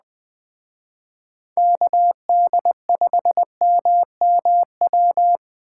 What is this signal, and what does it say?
Morse code, heard as a keyed tone, about 20 wpm, 700 Hz. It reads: KD5MMW